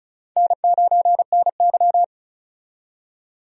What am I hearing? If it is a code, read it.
Morse code, heard as a keyed tone, about 36 wpm, 700 Hz. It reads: N9NY
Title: USA Call Signs - 4 Characters - 35wpm